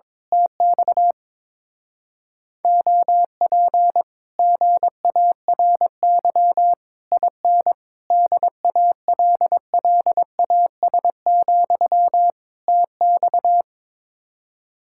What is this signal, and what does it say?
Morse code, heard as a keyed tone, about 21 wpm, 700 Hz. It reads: TX OP GARY IN DALLAS, TX
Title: Speed-Racing Common QSO Phrases - 22wpm